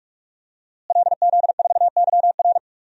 Morse code, heard as a keyed tone, about 43 wpm, 700 Hz. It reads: LZ4YR
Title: DXCC-100 Call Signs - 5 Characters - Repeated 3x - 45wpm